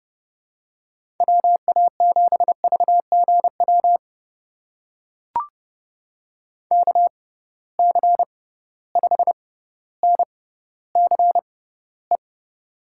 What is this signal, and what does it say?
Morse code, heard as a keyed tone, about 30 wpm, spaced as for 15 wpm, 700 Hz. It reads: WA7VGW KC5NCE
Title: USA Call Signs - 6 Characters - 6x Character Spacing - 30wpm